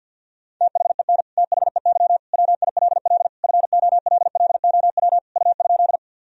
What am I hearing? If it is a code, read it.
Morse code, heard as a keyed tone, about 50 wpm, 700 Hz. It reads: THEN THEY WILL FOLLOW U?
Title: Sentences from Top 200 Words and Abbreviations 50wpm